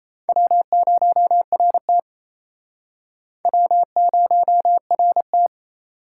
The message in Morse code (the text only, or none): W0RT W0RT